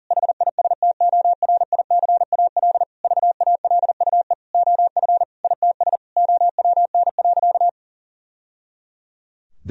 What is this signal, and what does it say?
Morse code, heard as a keyed tone, about 40 wpm, 700 Hz. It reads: HISTORICAL VALUE OF ITS OWN.
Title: Sentences from Top 1000 Words - Review Part 4 - 40wpm